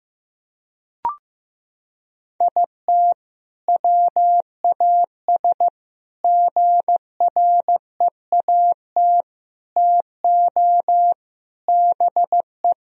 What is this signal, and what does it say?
Morse code, heard as a keyed tone, about 15 wpm, 700 Hz. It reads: IT WAS GREAT TO BE
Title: Sentences from Top 1000 Words - Review Part 2 - 15wpm